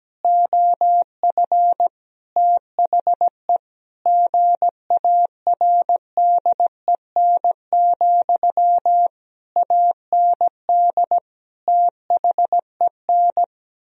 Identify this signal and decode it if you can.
Morse code, heard as a keyed tone, about 17 wpm, 700 Hz. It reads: OF THE GARDEN, AND THEN
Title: Sentences from Top 1000 Words - Review Part 4 - 17wpm